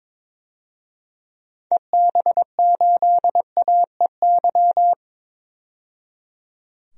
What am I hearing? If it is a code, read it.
Morse code, heard as a keyed tone, about 22 wpm, 700 Hz. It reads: EB8AEY
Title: DXCC-100 Call Signs - 6 Characters - 22wpm